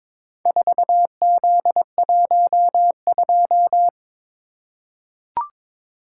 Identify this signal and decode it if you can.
Morse code, heard as a keyed tone, about 23 wpm, 700 Hz. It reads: 4Z12
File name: practice / 22wpm / Four Letter-Number Combinations - 22wpm